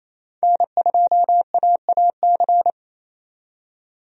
Morse code, heard as a keyed tone, about 27 wpm, 700 Hz. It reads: N2AAC
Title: USA Call Signs - 5 Characters - 28wpm